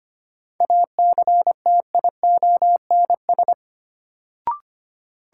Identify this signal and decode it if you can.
Morse code, heard as a keyed tone, about 25 wpm, 700 Hz. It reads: ACTIONS
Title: Sentences from Top 1000 Words - Review Part 6 - New Word Drill - 25wpm